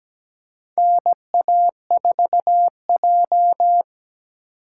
Morse code, heard as a keyed tone, about 17 wpm, 700 Hz. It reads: NA4J
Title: USA Call Signs - 4 Characters - 4x Character Spacing - 17wpm